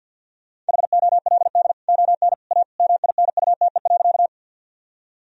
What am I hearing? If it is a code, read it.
Morse code, heard as a keyed tone, about 50 wpm, 700 Hz. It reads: HOLD ON A MINUTE.
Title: Sentences from Top 1000 Words - Review Part 2 - 50wpm